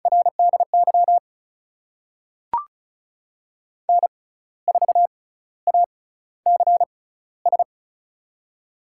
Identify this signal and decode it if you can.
Morse code, heard as a keyed tone, about 36 wpm, spaced as for 15 wpm, 700 Hz. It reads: RDY N4ACS
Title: USA Call Signs - 5 Characters - 6x Character Spacing - 35wpm